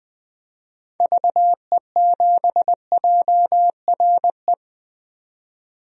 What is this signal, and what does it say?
Morse code, heard as a keyed tone, about 20 wpm, 700 Hz. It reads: VE7JRE
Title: DXCC-100 Call Signs - 6 Characters - 20wpm